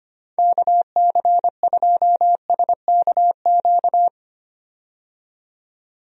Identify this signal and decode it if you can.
Morse code, heard as a keyed tone, about 25 wpm, 700 Hz. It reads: KC2SKQ
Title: USA Call Signs - 6 Characters - 25wpm